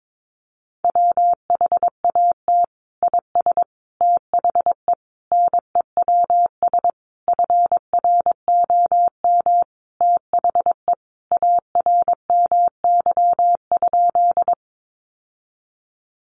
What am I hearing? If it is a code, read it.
Morse code, heard as a keyed tone, about 22 wpm, 700 Hz. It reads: WHAT IS THE NEWS FROM THE ARMY?
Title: Sentences from Top 800 Words 22wpm